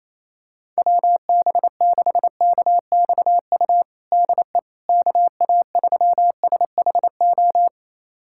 Morse code, heard as a keyed tone, about 28 wpm, 700 Hz. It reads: WB6KXU DE KA3SHO
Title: USA Callsign DE Callsign 28wpm